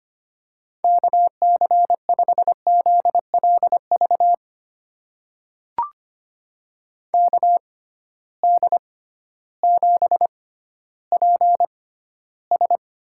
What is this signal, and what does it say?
Morse code, heard as a keyed tone, about 25 wpm, spaced as for 13 wpm, 700 Hz. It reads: KC5ZLV KD7PS
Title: USA Call Signs - 6 Characters - 6x Character Spacing - 25wpm